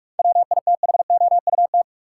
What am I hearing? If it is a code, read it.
Morse code, heard as a keyed tone, about 42 wpm, 700 Hz. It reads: WITHOUT
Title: Sentences from Top 600 Words - New Word Drill - 45wpm